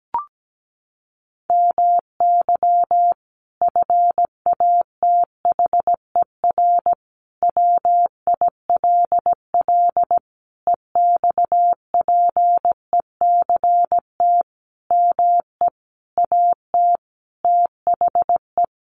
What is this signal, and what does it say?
Morse code, heard as a keyed tone, about 17 wpm, 700 Hz. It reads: MY FATHER WILL EXPECT ME AT THE
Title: Sentences from Top 1100 Words 17wpm